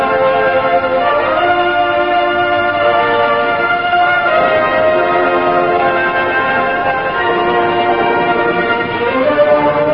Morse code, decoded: none